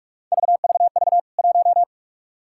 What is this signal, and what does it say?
Morse code, heard as a keyed tone, about 46 wpm, 700 Hz. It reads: VVV 1
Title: K1USN SST QSO Practice 45wpm